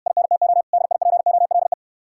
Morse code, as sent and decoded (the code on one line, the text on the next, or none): .. -. - --- / -.. .. .--. --- .-.. .
INTO DIPOLE